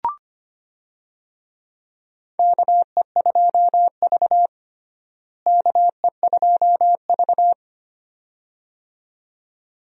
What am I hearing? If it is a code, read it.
Morse code, heard as a keyed tone, about 25 wpm, 700 Hz. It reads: KE2V KE2V
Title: USA Call Signs - 4 Characters - Repeated 2x - 25wpm